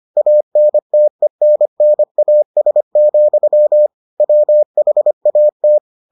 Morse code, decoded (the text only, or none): ANTENNAS, WHAT